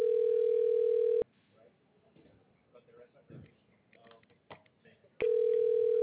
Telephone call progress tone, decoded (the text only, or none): ringback tone